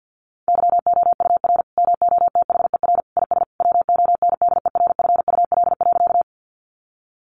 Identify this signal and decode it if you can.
Morse code, heard as a keyed tone, about 48 wpm, 700 Hz. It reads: YOUR MOTHER IS WONDERFUL.
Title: Sentences from Top 1100 Words 50wpm